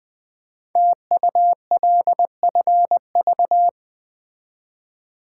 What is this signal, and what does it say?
Morse code, heard as a keyed tone, about 20 wpm, 700 Hz. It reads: TULFV
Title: Code Groups - 5 Letters - 20wpm